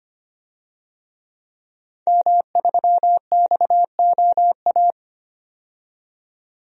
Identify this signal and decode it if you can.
Morse code, heard as a keyed tone, about 25 wpm, 700 Hz. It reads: M3XOA
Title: DXCC-100 Call Signs - 5 Characters - 25wpm